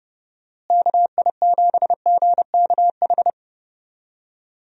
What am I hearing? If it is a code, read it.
Morse code, heard as a keyed tone, about 30 wpm, 700 Hz. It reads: KI7GKH
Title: USA Call Signs - 6 Characters - Repeated 2x - 30wpm